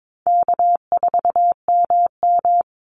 Morse code, heard as a keyed tone, about 22 wpm, 700 Hz. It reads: K4MM
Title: Easy Call Signs Repeated 2x - 22wpm